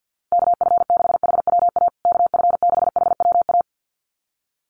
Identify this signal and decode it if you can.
Morse code, heard as a keyed tone, about 50 wpm, 700 Hz. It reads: KF6HWA KF6HWA
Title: USA Call Signs Repeated 2x 50wpm